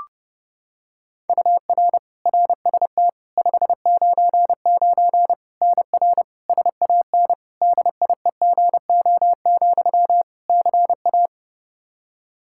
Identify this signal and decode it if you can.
Morse code, heard as a keyed tone, about 30 wpm, 700 Hz. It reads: UR RST 599 NR SAN DIEGO, CA
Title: Practice QSO - Exchange 1 - Beginning - 30wpm